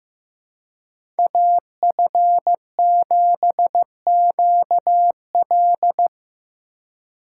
Morse code, heard as a keyed tone, about 15 wpm, 700 Hz. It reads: AF7QL
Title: USA Call Signs - 5 Characters - 2x Character Spacing - 15wpm